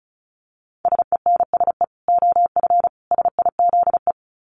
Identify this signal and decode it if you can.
Morse code, heard as a keyed tone, about 34 wpm, 700 Hz. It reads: SENSE OF SIZE
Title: Dit Challenge 35wpm